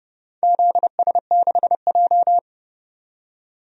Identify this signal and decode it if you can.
Morse code, heard as a keyed tone, about 29 wpm, 700 Hz. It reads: ZS6J
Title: DXCC-100 Call Signs - 4 Characters - 30wpm